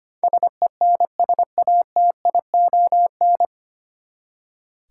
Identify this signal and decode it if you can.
Morse code, heard as a keyed tone, about 25 wpm, 700 Hz. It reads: SENSATION
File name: Nine Letter Words - 25wpm